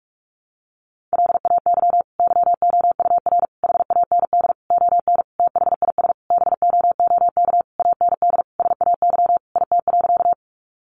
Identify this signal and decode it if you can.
Morse code, heard as a keyed tone, about 44 wpm, 700 Hz. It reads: LAY YOUR HAND ON THIS BOOK AND SAY IT.
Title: Sentences from Top 400 Words 45wpm